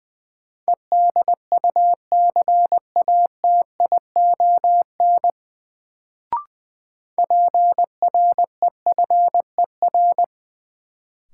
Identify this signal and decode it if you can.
Morse code, heard as a keyed tone, about 20 wpm, 700 Hz. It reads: EDUCATION PREFER
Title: Sentences from Top 1100 Words - New Word Drill Alt - 20wpm